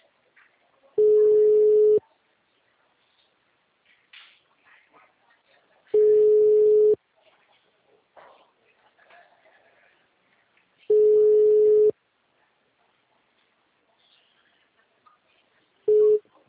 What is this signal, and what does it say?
Ringback tone (Europe, 425 Hz).
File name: out- -2013